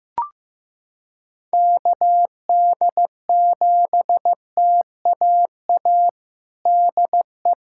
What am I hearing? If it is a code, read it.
Morse code, heard as a keyed tone, about 16 wpm, 700 Hz. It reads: KD7TAA DE
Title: USA Callsign DE Callsign 15wpm